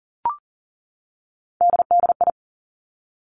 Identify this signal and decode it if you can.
Morse code, heard as a keyed tone, about 38 wpm, 700 Hz. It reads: DDI
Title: Three Challenging Letter-Number Combinations 40wpm